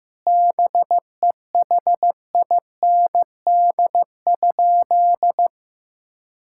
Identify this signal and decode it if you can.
Morse code, heard as a keyed tone, about 16 wpm, 700 Hz. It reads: BEHIND?
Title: Sentences from Top 1000 Words - Review Part 5 - 15wpm